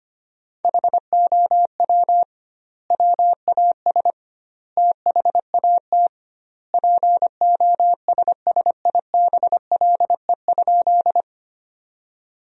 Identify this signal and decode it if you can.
Morse code, heard as a keyed tone, about 25 wpm, 700 Hz. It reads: HOW WAS THAT POSSIBLE?
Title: 2x Word-Spacing - Sentences from Top 500 Words - 25wpm